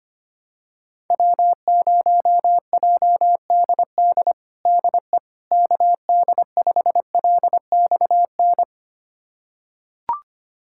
Morse code, heard as a keyed tone, about 25 wpm, 700 Hz. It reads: W0JDD DE KD5LXN
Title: USA Callsign DE Callsign 25wpm